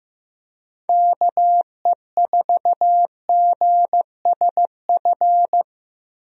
Morse code, heard as a keyed tone, about 16 wpm, 700 Hz. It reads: KE4GSF
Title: USA Call Signs - 6 Characters - Repeated 2x - 15wpm